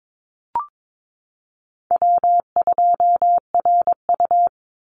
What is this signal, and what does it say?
Morse code, heard as a keyed tone, about 22 wpm, 700 Hz. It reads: W2RU
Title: Call Signs Repeated 3x - 22wpm